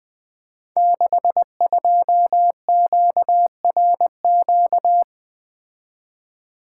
Morse code, heard as a keyed tone, about 20 wpm, 700 Hz. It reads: 62QRQ
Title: Code Groups - 5 Letter:Numbers - 20wpm